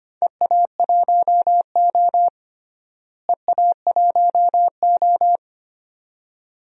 Morse code, heard as a keyed tone, about 25 wpm, 700 Hz. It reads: EA1O EA1O
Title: DXCC-100 Call Signs - 4 Characters - Repeated 2x - 25wpm